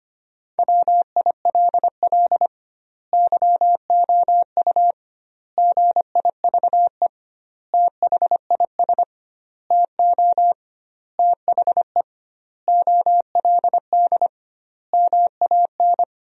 Morse code, heard as a keyed tone, about 25 wpm, 700 Hz. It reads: WILL YOU GIVE THIS TO THE OLD MAN
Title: 2x Word-Spacing - Sentences from Top 200 Words - 25wpm